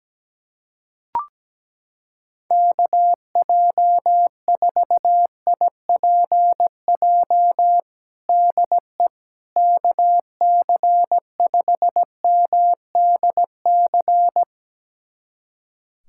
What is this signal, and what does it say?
Morse code, heard as a keyed tone, about 17 wpm, 700 Hz. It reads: KJ4IPJ DE KC5MDC
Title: USA Callsign DE Callsign - 6 Characters - 17wpm